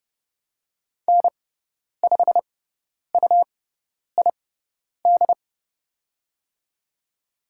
Morse code, heard as a keyed tone, about 29 wpm, spaced as for 10 wpm, 700 Hz. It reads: N5UID
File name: Lesson 18g - Letter m - Call Signs 30wpm